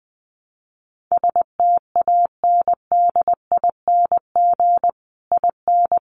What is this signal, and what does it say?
Morse code, heard as a keyed tone, about 20 wpm, 700 Hz. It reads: STANDING IN
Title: Sets of 5 Words Encore2 20wpm